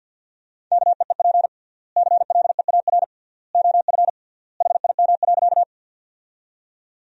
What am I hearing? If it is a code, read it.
Morse code, heard as a keyed tone, about 51 wpm, 700 Hz. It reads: KEEP CLEAR OF HIM.
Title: 3x Word-Spacing - Sentences from Top 500 Words - 50wpm